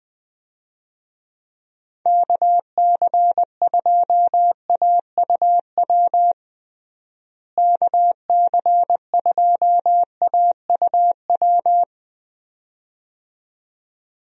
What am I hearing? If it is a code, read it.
Morse code, heard as a keyed tone, about 20 wpm, 700 Hz. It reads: KC2AUW KC2AUW
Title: USA Call Signs - 6 Characters - Repeated 2x - 20wpm